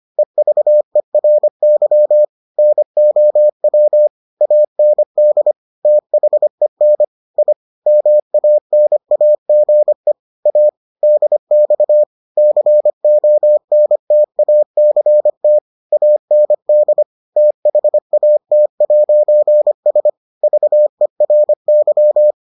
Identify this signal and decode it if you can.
Morse code, heard as a keyed tone, about 25 wpm, 600 Hz. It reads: EVERY NOW AND THEN I MANAGE A DX CONTACT AND THAT'S VERY